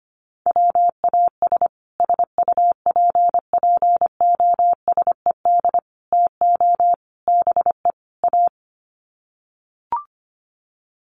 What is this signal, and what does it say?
Morse code, heard as a keyed tone, about 25 wpm, 700 Hz. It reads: WAS SUPPOSED TO BE A
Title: Sets of 5 Words Encore 25wpm